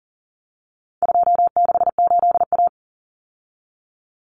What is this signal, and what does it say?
Morse code, heard as a keyed tone, about 40 wpm, 700 Hz. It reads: 268A
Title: Common Cut Numbers - 4-Digit 40wpm